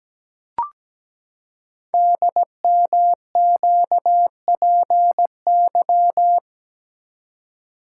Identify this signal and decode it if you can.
Morse code, heard as a keyed tone, about 17 wpm, 700 Hz. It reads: DMQPY
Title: Code Groups - 5 Letters - 17wpm